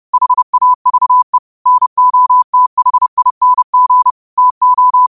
Morse code, heard as a keyed tone, about 30 wpm, 1000 Hz. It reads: HAVE NOTHING TO
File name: qotdfiles / 30wpm